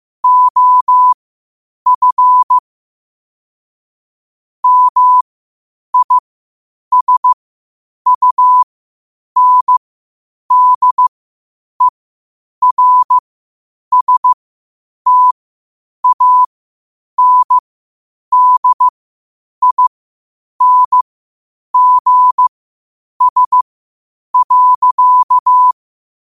Morse code, heard as a keyed tone, about 15 wpm, spaced as for 8 wpm, 1000 Hz. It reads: OF MISUNDERSTANDINGS.
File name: qotdfiles / 8wpm